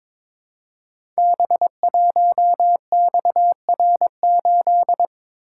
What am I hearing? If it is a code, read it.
Morse code, heard as a keyed tone, about 22 wpm, 700 Hz. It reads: B1XR8
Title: Code Groups - 5 Letter:Numbers - 22wpm